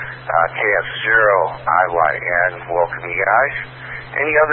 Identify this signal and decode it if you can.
Emergency Alert System (SAME) tones, heard as harsh buzzing bursts, with speech. It is heard from 0.9 to 3.9 s.